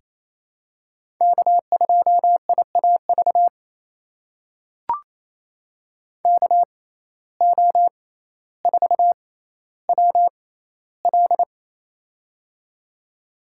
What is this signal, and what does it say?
Morse code, heard as a keyed tone, about 28 wpm, spaced as for 14 wpm, 700 Hz. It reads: K2IAV KO4WL
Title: USA Call Signs - 5 Characters - 6x Character Spacing - 28wpm